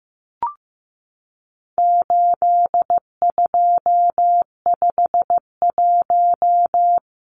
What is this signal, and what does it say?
Morse code, heard as a keyed tone, about 15 wpm, 700 Hz. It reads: 8251
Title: Numbers - 4-Digit 15wpm